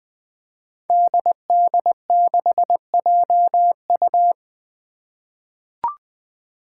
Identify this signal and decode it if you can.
Morse code, heard as a keyed tone, about 20 wpm, 700 Hz. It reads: DD6JU
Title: Five Challenging Letter-Number Combinations 20wpm